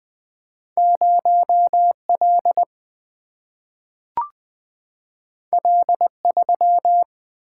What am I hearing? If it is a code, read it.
Morse code, heard as a keyed tone, about 20 wpm, 700 Hz. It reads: 0 L L 3
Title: Single-Letter Single-Number 20wpm